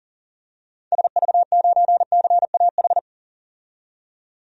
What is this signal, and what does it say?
Morse code, heard as a keyed tone, about 38 wpm, 700 Hz. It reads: SV9CAH
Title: DXCC-100 Call Signs - 6 Characters - 40wpm